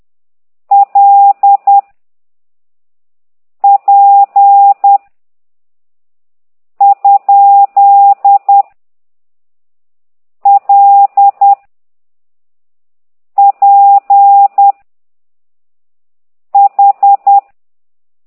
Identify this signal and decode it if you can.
Morse code, heard as a keyed tone, about 10 wpm, 800 Hz. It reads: L P ? L P H
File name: ct2idw / exerc15